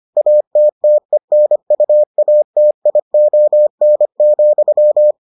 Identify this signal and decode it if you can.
Morse code, heard as a keyed tone, about 25 wpm, 600 Hz. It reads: ATTENUATION,